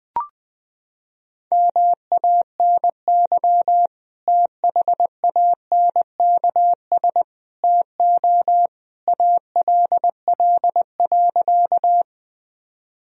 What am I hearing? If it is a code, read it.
Morse code, heard as a keyed tone, about 20 wpm, 700 Hz. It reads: MANY THANKS TO ALL.
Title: Sentences from Top 1000 Words - Review Part 3 - 20wpm